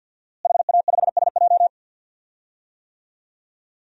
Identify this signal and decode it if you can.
Morse code, heard as a keyed tone, about 51 wpm, 700 Hz. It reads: HA5SJ
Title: DXCC-100 Call Signs - 5 Characters - Repeated 3x - 50wpm